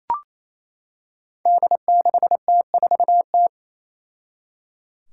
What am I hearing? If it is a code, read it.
Morse code, heard as a keyed tone, about 28 wpm, 700 Hz. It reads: D6T4T
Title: Five Challenging Letter-Number Combinations 28wpm